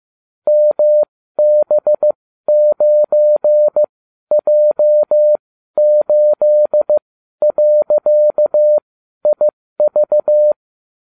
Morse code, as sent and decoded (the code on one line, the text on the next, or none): -- -... ----. .--- ---.. .-.-.- .. ...-
MB9J8.IV